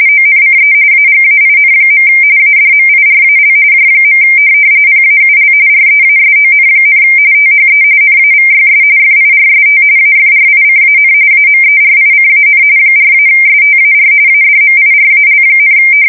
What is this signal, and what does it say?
RTTY, about 45.45 baud. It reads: ONCE AGAIN. ADMIRAL J. PAUL REASON, THE NAVYS FIRST AFRICAN AMERICAN 4-STAR ADMIRAL,
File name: fsk170 / USS-New-Jersey